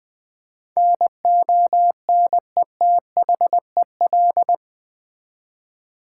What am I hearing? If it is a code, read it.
Morse code, heard as a keyed tone, about 20 wpm, 700 Hz. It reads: NONETHEL
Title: Build A Word - 4 Letters At A Time - 11-Letter Words - 20wpm